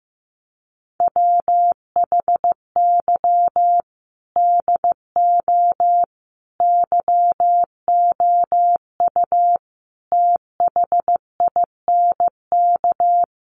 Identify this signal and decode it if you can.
Morse code, heard as a keyed tone, about 15 wpm, 700 Hz. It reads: WHY DO YOU THINK
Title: Sets of 5 Words Encore2 15wpm